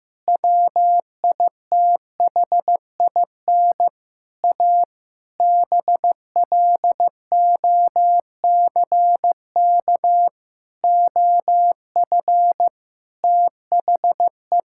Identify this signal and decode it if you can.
Morse code, heard as a keyed tone, about 15 wpm, 700 Hz. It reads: WITHIN A BLOCK OF THE
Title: Sentences from Top 1600 Words 15wpm